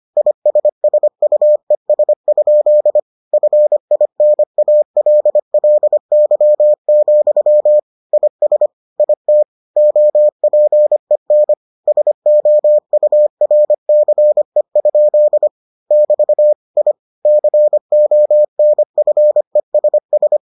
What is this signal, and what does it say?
Morse code, heard as a keyed tone, about 25 wpm, 600 Hz. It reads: ISSUES? FINALLY, IS IT OPEN SOURCE? <BT> I CONFESS